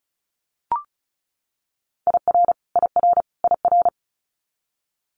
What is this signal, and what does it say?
Morse code, heard as a keyed tone, about 36 wpm, 700 Hz. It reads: IR IR IR
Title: Multi-letter Phonemes Repeated 3x 35wpm